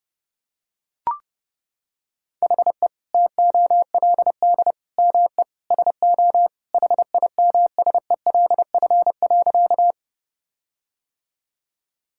Morse code, heard as a keyed tone, about 30 wpm, 700 Hz. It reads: HE TOLD ME SO HIMSELF.
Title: Sentences from Top 1600 Words 30wpm